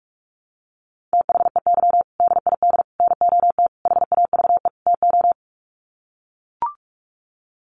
Morse code, heard as a keyed tone, about 44 wpm, 700 Hz. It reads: THEY DID NOT HAVE TO
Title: Sets of 5 Words Encore2 45wpm